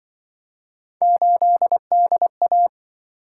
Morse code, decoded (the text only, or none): 8DA